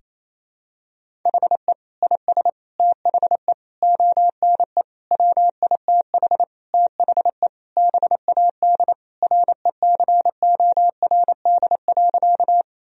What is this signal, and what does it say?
Morse code, heard as a keyed tone, about 28 wpm, 700 Hz. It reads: HE IS THE ONE WITH THE BAD RECORD.
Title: Sentences from Top 1000 Words - Review Part 3 - 28wpm